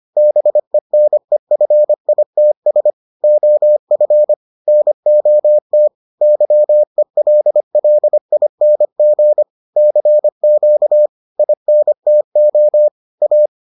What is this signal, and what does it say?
Morse code, heard as a keyed tone, about 25 wpm, 600 Hz. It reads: BENEFITS OF NOT YELLING CQ INTO A